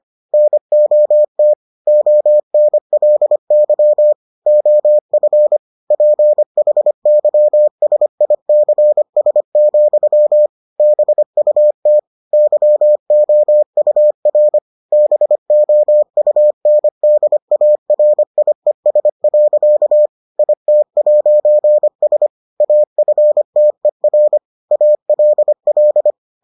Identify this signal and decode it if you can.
Morse code, heard as a keyed tone, about 25 wpm, 600 Hz. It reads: NOT ONLY OF PHYSICS, BUT YOUR BOUNDARIES. IT'S AFTER ALL